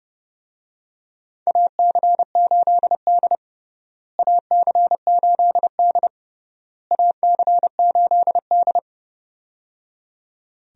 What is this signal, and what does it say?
Morse code, heard as a keyed tone, about 30 wpm, 700 Hz. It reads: AC8D AC8D AC8D
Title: USA Call Signs - 4 Characters - Repeated 3x - 30wpm